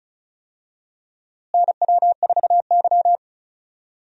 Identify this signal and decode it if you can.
Morse code, heard as a keyed tone, about 36 wpm, 700 Hz. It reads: NW4Y
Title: USA Call Signs - 4 Characters - Repeated 2x - 35wpm